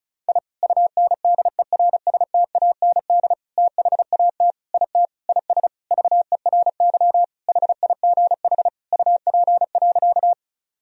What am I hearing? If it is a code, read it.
Morse code, heard as a keyed tone, about 36 wpm, 700 Hz. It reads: I UNDERSTAND THAT IT IS VERY HIGH UP.
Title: Sentences from Top 1000 Words - Review Part 1 - 35wpm